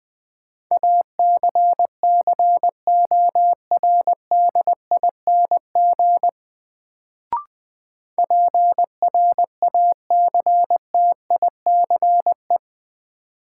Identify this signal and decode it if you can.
Morse code, heard as a keyed tone, about 20 wpm, 700 Hz. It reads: ACCORDING PRACTICE
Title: Sentences from Top 700 Words - New Word Drill - 20wpm